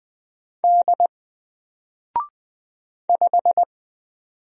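Morse code, heard as a keyed tone, about 20 wpm, 700 Hz. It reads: D 5
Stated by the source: Single letter and number Morse drill at 20 wpm, from N4LCD.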